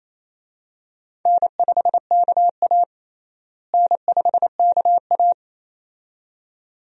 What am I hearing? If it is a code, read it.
Morse code, heard as a keyed tone, about 28 wpm, 700 Hz. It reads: N5KA N5KA
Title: USA Call Signs - 4 Characters - Repeated 2x - 28wpm